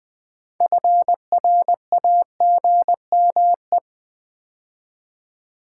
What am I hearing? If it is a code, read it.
Morse code, heard as a keyed tone, about 20 wpm, 700 Hz. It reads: FRAGME
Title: Build A Word - 2 Letters At A Time - 8-Letter Words - 20wpm